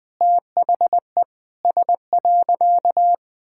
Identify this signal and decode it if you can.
Morse code, heard as a keyed tone, about 20 wpm, 700 Hz. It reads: THE S.
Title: Sentences with 6-Word Phrases - 20wpm